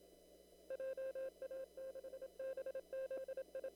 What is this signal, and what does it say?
Morse code, heard as a keyed tone, about 27 wpm, 530 Hz. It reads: JA6BZI
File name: CW / kt5j wpx cw-op-k5pi